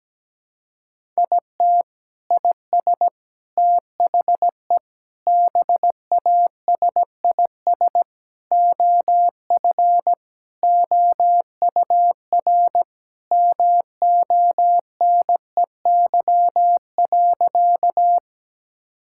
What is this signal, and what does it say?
Morse code, heard as a keyed tone, about 17 wpm, 700 Hz. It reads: IT IS THE BASIS OF OUR MONEY.
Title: Sentences from Top 1000 Words - Review Part 5 - 17wpm